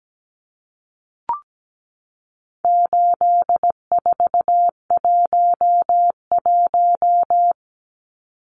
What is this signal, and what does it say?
Morse code, heard as a keyed tone, about 17 wpm, 700 Hz. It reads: 8411
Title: Numbers - 4-Digit 17wpm